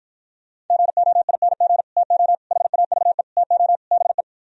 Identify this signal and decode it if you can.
Morse code, heard as a keyed tone, about 52 wpm, 700 Hz. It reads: GOING TO HAVE TO BE